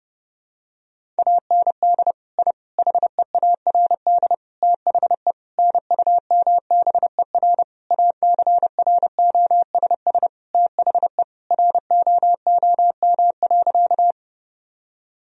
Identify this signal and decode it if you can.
Morse code, heard as a keyed tone, about 32 wpm, 700 Hz. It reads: AND I HEARD THE NUMBER ACROSS THE ROOM.
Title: Sentences from Top 1000 Words - Review Part 1 - 30wpm